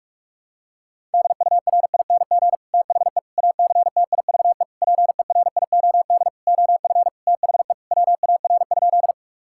Morse code, heard as a keyed tone, about 46 wpm, 700 Hz. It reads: DURING THE ACTIVE PERIOD OF THE WAR?
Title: Sentences from Top 1000 Words - Review Part 6 - 45wpm